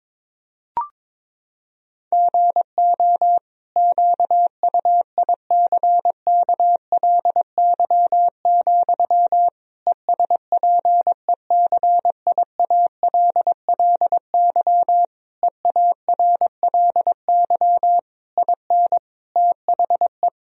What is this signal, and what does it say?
Morse code, heard as a keyed tone, about 21 wpm, 700 Hz. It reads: GO QUICKLY, ESPECIALLY EARLY IN THE
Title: Sentences from Top 1000 Words - Review Part 4 - 22wpm